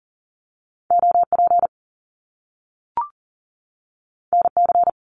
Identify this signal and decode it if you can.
Morse code, heard as a keyed tone, about 40 wpm, 700 Hz. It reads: OP NC
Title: Top Two Letter Combinations in Words 40wpm